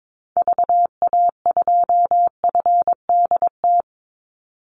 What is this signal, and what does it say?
Morse code, heard as a keyed tone, about 22 wpm, 700 Hz. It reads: VA2FDT
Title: Call Signs - 22wpm